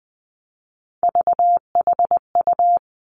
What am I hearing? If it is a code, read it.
Morse code, heard as a keyed tone, about 20 wpm, 700 Hz. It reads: VHU
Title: Three Challenging Letter-Number Combinations 20wpm